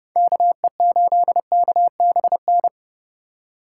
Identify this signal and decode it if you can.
Morse code, heard as a keyed tone, about 31 wpm, 700 Hz. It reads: KE8KBN
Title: USA Call Signs - 2x Character Spacing - 30wpm